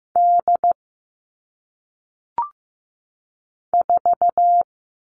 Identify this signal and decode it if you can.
Morse code, heard as a keyed tone, about 15 wpm, 700 Hz. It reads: D 4
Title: All Cut Numbers - 1-Digit 15wpm